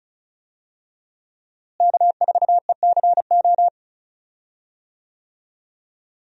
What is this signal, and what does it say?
Morse code, heard as a keyed tone, about 34 wpm, 700 Hz. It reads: K4ECO
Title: USA Call Signs - 5 Characters - Repeated 3x - 35wpm